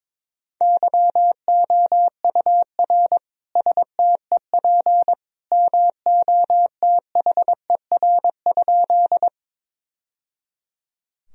Morse code, heard as a keyed tone, about 22 wpm, 700 Hz. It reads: YOUR STEP MOTHER?
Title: Sentences from Top 1000 Words - Review Part 2 - 22wpm